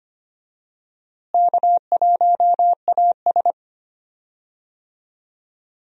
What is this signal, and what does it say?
Morse code, heard as a keyed tone, about 25 wpm, 700 Hz. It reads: K1AS
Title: USA Call Signs - 4 Characters - Repeated 3x - 25wpm